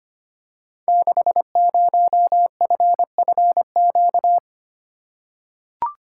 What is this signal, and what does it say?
Morse code, heard as a keyed tone, about 25 wpm, 700 Hz. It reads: 60FFQ
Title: Code Groups - 5 Letter:Numbers - 25wpm